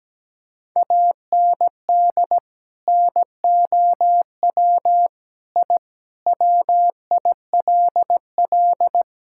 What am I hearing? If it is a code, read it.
Morse code, heard as a keyed tone, about 17 wpm, 700 Hz. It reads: AND NOW I WILL
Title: Sentences from Top 1000 Words - Review Part 6 - 17wpm